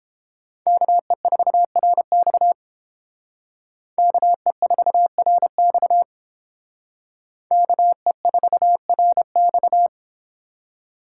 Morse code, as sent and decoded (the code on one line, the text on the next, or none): -.- . ....- .-. -..- / -.- . ....- .-. -..- / -.- . ....- .-. -..-
KE4RX KE4RX KE4RX